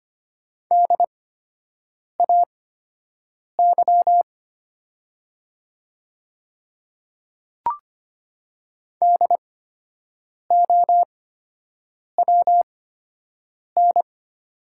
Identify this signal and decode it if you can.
Morse code, heard as a keyed tone, about 25 wpm, spaced as for 7 wpm, 700 Hz. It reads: DAY DOWN
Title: 8x Character Spacing - Top 100 Words - 25wpm